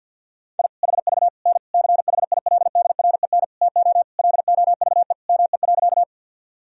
Morse code, heard as a keyed tone, about 51 wpm, 700 Hz. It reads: I HV N CHILDREN TO LOVE ME.
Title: Sentences from Top 400 Words and Abbreviations 50wpm